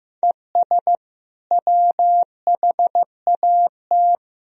Morse code, heard as a keyed tone, about 15 wpm, 700 Hz. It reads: ES WHAT
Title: Sentences from Top 100 Words and Abbreviations 15wpm